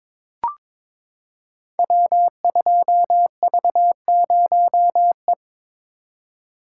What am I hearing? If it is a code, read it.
Morse code, heard as a keyed tone, about 22 wpm, 700 Hz. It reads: W2V0E
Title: Code Groups - 5 Letter:Numbers - 22wpm